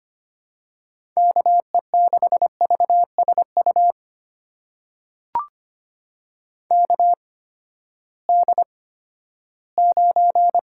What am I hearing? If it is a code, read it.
Morse code, heard as a keyed tone, about 25 wpm, spaced as for 14 wpm, 700 Hz. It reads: KE6VSU KD9
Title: USA Call Signs - 6 Characters - 8x Character Spacing - 25wpm